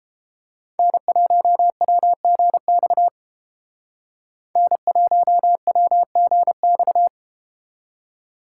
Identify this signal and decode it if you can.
Morse code, heard as a keyed tone, about 31 wpm, 700 Hz. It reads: N1WGX N1WGX